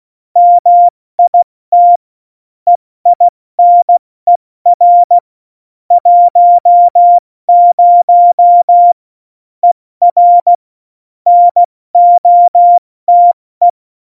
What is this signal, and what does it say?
Morse code, heard as a keyed tone, about 16 wpm, spaced as for 14 wpm, 700 Hz. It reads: MIT EINER 10ER NOTE
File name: Morsekurs / Witze / Witz 15. Oktober 2019, 70 F80